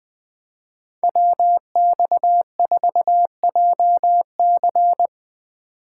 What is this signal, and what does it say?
Morse code, heard as a keyed tone, about 20 wpm, 700 Hz. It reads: WX4JC
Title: USA Call Signs - 5 Characters - 3x Character Spacing - 20wpm